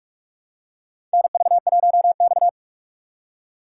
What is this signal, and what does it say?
Morse code, heard as a keyed tone, about 46 wpm, 700 Hz. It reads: NV1X
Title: USA Call Signs - 4 Characters - 6x Character Spacing - 45wpm